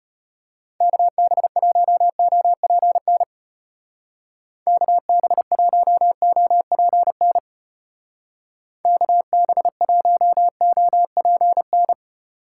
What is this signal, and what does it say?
Morse code, heard as a keyed tone, about 32 wpm, 700 Hz. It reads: KB1OPN KB1OPN KB1OPN